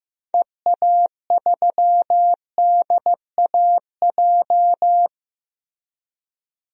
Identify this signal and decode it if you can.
Morse code, heard as a keyed tone, about 15 wpm, 700 Hz. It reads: EA3DAJ
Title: DXCC-100 Call Signs - 6 Characters - Repeated 3x - 15wpm